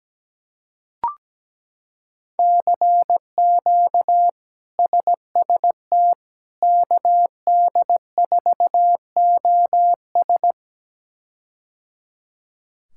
Morse code, heard as a keyed tone, about 17 wpm, 700 Hz. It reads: CQ SST KD4OS
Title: K1USN SST QSO Practice - 17wpm